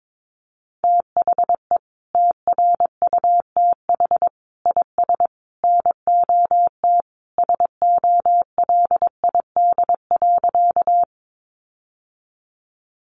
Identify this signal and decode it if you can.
Morse code, heard as a keyed tone, about 22 wpm, 700 Hz. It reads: THE TRUTH IS NOT SOLID.
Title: Sentences from Top 1500 Words 22wpm